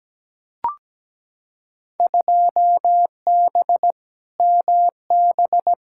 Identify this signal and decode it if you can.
Morse code, heard as a keyed tone, about 17 wpm, 700 Hz. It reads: 2B MB
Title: Field Day Exchanges Encore - 17wpm